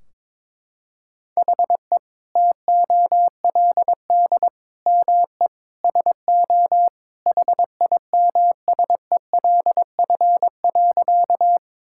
Morse code, heard as a keyed tone, about 22 wpm, 700 Hz. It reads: HE TOLD ME SO HIMSELF.
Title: Sentences from Top 1600 Words 22wpm